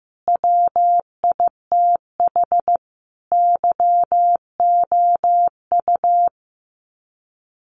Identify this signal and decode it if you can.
Morse code, heard as a keyed tone, about 15 wpm, 700 Hz. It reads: WITH YOU
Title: Sets of 5 Words Encore2 15wpm